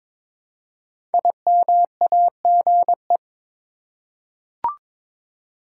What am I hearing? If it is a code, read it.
Morse code, heard as a keyed tone, about 22 wpm, 700 Hz. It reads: IMAGE
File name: 5 Letter Words - First 3 Letters Given - 22wpm